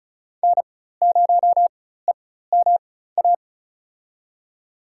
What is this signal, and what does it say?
Morse code, heard as a keyed tone, about 34 wpm, spaced as for 16 wpm, 700 Hz. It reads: N0EMA
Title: USA Call Signs - 5 Characters - 4x Character Spacing - 35wpm